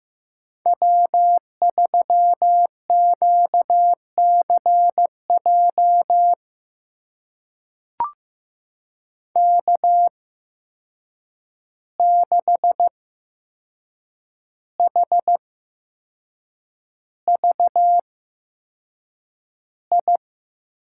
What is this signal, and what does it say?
Morse code, heard as a keyed tone, about 15 wpm, spaced as for 6 wpm, 700 Hz. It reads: W3QCJ K6HVI
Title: USA Call Signs - 5 Characters - 8x Character Spacing - 15wpm